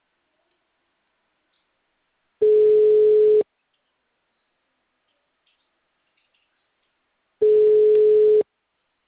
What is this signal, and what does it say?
Ringback tone (Europe, 425 Hz).